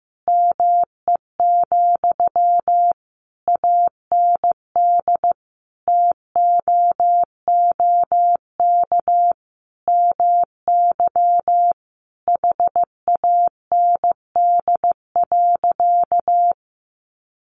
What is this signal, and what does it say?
Morse code, heard as a keyed tone, about 15 wpm, 700 Hz. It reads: ME, AND TOOK MY HAND.
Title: Sentences from Top 1100 Words 15wpm